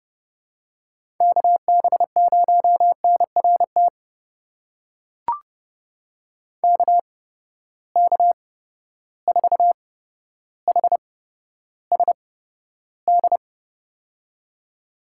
Morse code, heard as a keyed tone, about 30 wpm, spaced as for 13 wpm, 700 Hz. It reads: KB0NRT KK4HSD
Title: USA Call Signs - 6 Characters - 8x Character Spacing - 30wpm